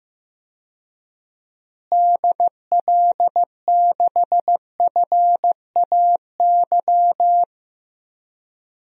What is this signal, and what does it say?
Morse code, heard as a keyed tone, about 15 wpm, 700 Hz. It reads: DL6FAY
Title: DXCC-100 Call Signs - 6 Characters - Repeated 3x - 15wpm